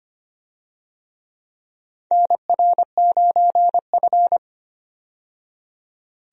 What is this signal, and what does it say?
Morse code, heard as a keyed tone, about 25 wpm, 700 Hz. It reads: NR9F
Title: USA Call Signs - 4 Characters - 25wpm